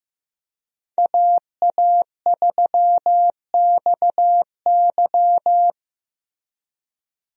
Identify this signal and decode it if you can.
Morse code, heard as a keyed tone, about 16 wpm, 700 Hz. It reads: AA3XY
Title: USA Call Signs - 5 Characters - Repeated 2x - 15wpm